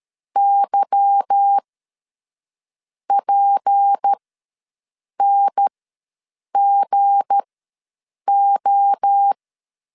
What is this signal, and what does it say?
Morse code, heard as a keyed tone, about 13 wpm, spaced as for 6 wpm, 800 Hz. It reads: YPNGO
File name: LEZIONI / 8 YJ45 / 8 YJ4 ws 12wpm 3249